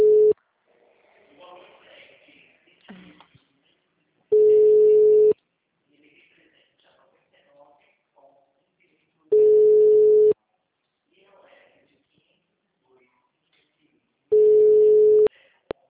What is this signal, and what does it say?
Ringback tone (Europe, 425 Hz), with speech.